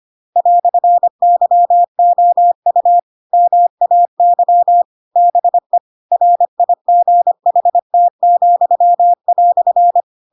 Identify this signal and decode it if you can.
Morse code, heard as a keyed tone, about 25 wpm, 700 Hz. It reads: "YOU MAY BE RIGHT,"
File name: Pinocchio / Pinocchio ch7 25 25 ik1hge